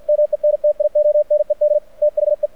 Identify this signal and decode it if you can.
Morse code, heard as a keyed tone, about 50 wpm, 600 Hz, with radio noise. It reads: MENTIONED THE
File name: archived / CRSnet